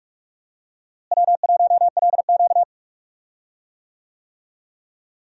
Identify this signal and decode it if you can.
Morse code, heard as a keyed tone, about 46 wpm, 700 Hz. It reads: W1LQ
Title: USA Call Signs - 4 Characters - Repeated 3x - 45wpm